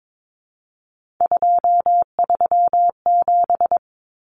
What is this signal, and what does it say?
Morse code, heard as a keyed tone, about 22 wpm, 700 Hz. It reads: 237
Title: Numbers - 3-Digit 22wpm